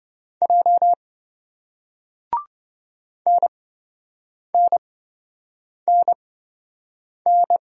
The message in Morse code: J NNNN

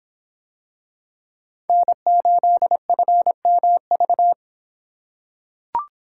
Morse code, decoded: N8FMV

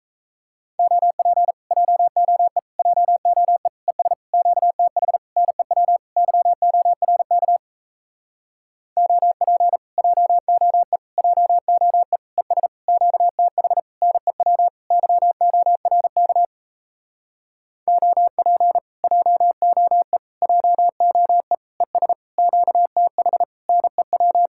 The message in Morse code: OP JOE JOE ES QTH NEW YORK OP JOE JOE ES QTH NEW YORK OP JOE JOE ES QTH NEW

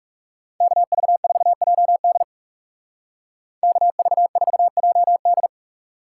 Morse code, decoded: KV4JD KV4JD

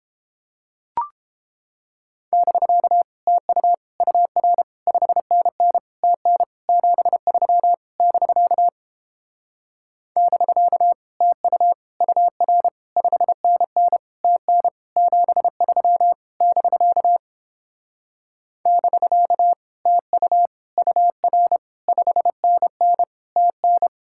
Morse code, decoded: <BK> TU UR 5NN TN 73 <BK> <BK> TU UR 5NN TN 73 <BK> <BK> TU UR 5NN TN

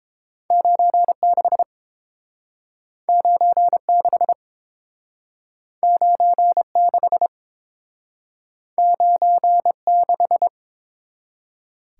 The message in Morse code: 96 96 96 96

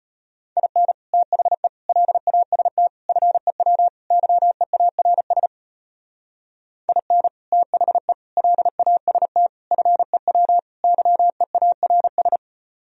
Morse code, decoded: IN THE LAST FEW YEARS IN THE LAST FEW YEARS

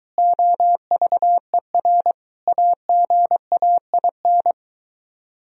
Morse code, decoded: OVER AGAIN

